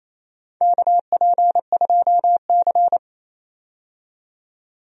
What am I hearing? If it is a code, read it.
Morse code, heard as a keyed tone, about 28 wpm, 700 Hz. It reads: KP2C
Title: DXCC-100 Call Signs - 4 Characters - 28wpm